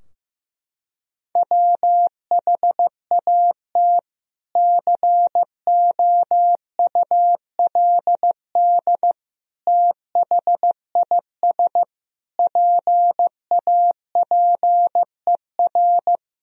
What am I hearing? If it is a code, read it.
Morse code, heard as a keyed tone, about 15 wpm, 700 Hz. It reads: WHAT COULD THIS PAPER
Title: Sentences from Top 1000 Words - Review Part 2 - 15wpm